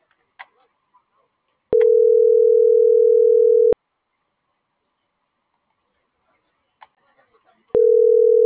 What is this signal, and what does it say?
Ringback tone (North America), 440 + 480 Hz.